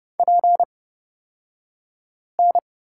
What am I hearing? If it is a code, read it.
Morse code, heard as a keyed tone, about 29 wpm, spaced as for 5 wpm, 700 Hz. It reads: PN
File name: Lesson 38d - Letter q - Two Characters 30wpm